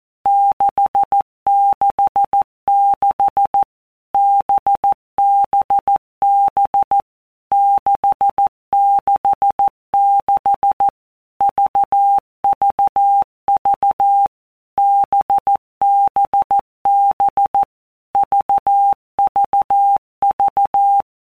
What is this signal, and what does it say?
Morse code, heard as a keyed tone, about 14 wpm, 800 Hz. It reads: 666 BBB 666 VVV BBB VVV
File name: Basic Letter Groups By Blocks / Groups@13 / VB46@13